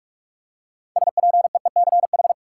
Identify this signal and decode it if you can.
Morse code, heard as a keyed tone, about 43 wpm, 700 Hz. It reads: SPEECH
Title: Six Letter Words - 45wpm